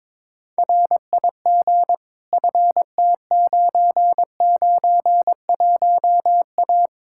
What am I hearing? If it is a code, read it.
Morse code, heard as a keyed tone, about 22 wpm, 700 Hz. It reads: RIG FT991A
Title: Practice QSO - Exchange 2 - Info - 22wpm